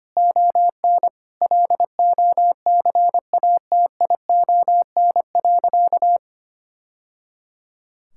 Morse code, heard as a keyed tone, about 25 wpm, 700 Hz. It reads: ON LOCATION.
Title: Sentences from Top 1000 Words - Review Part 6 - 25wpm